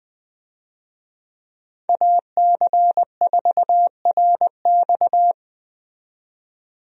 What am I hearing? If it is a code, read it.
Morse code, heard as a keyed tone, about 21 wpm, 700 Hz. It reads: AC4RX
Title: USA Call Signs - 5 Characters - 20wpm